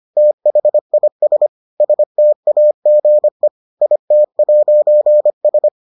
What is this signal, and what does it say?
Morse code, heard as a keyed tone, about 25 wpm, 600 Hz. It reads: THIS STAGE IT'S